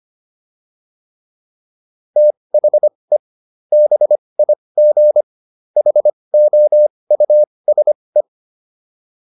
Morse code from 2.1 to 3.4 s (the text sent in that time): THE